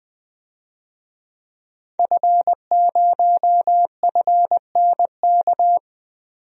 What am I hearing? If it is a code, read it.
Morse code, heard as a keyed tone, about 20 wpm, 700 Hz. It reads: F0FNK
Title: DXCC-100 Call Signs - 5 Characters - 20wpm